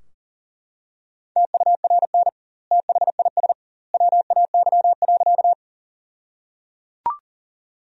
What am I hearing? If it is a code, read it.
Morse code, heard as a keyed tone, about 37 wpm, 700 Hz. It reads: TURN THIS WAY.
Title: 2x Word-Spacing - Sentences from Top 200 Words - 40wpm